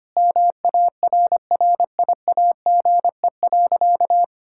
Morse code, heard as a keyed tone, about 25 wpm, 700 Hz. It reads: MARRIAGE.
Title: Sentences from Top 1000 Words - Review Part 5 - 25wpm